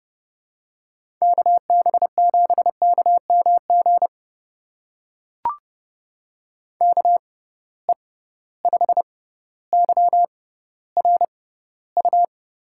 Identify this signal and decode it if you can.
Morse code, heard as a keyed tone, about 29 wpm, spaced as for 15 wpm, 700 Hz. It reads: KB7KMG KE5YRU
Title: USA Call Signs - 6 Characters - 6x Character Spacing - 30wpm